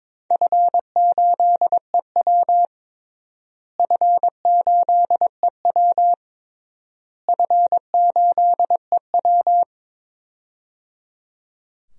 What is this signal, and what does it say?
Morse code, heard as a keyed tone, about 21 wpm, 700 Hz. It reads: F8EW F8EW F8EW
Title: DXCC-100 Call Signs - 4 Characters - Repeated 3x - 22wpm